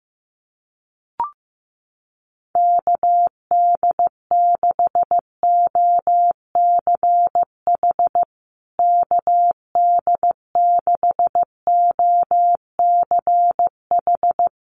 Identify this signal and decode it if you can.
Morse code, heard as a keyed tone, about 15 wpm, 700 Hz. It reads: KD6OCH KD6OCH
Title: USA Call Signs Repeated 3x 15wpm